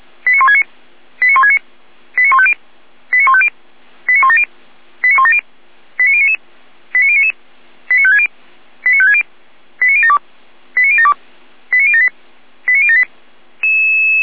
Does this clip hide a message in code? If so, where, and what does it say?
Yes: five-tone selective call (ZVEI) from 0.3 to 13.0 s, heard as quick runs of five tones, tones of about 70 ms. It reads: 89169 89169 89150 89150 89170 89170 89099 89099 89560 89560 89072 89072 89078 89078